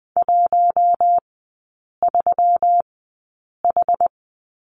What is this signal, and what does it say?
Morse code, heard as a keyed tone, about 20 wpm, spaced as for 8 wpm, 700 Hz. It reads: 13H
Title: 121 Lesson 20e - Number 3 - Three Characters 20wpm